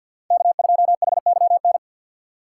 Morse code, heard as a keyed tone, about 52 wpm, 700 Hz. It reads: K2HYN
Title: USA Call Signs - 5 Characters - 2x Character Spacing - 50wpm